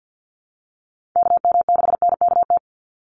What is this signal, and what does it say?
Morse code, heard as a keyed tone, about 50 wpm, 700 Hz. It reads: KM6NKT
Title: USA Call Signs 50wpm